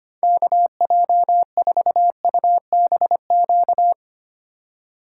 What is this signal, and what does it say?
Morse code, heard as a keyed tone, about 25 wpm, 700 Hz. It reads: KJ4UBQ
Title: USA Call Signs - 6 Characters - 4x Character Spacing - 25wpm